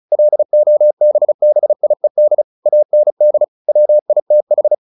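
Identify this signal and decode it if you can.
Morse code, heard as a keyed tone, about 35 wpm, 600 Hz. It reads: LOBBIED AND WITH
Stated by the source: Morse practice at 35 wpm from CW med Gnister, sending Ken Follett's novel Never in English.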